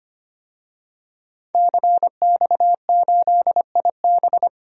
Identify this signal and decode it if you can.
Morse code, heard as a keyed tone, about 25 wpm, 700 Hz. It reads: CX8IB
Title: DXCC-100 Call Signs - 5 Characters - 25wpm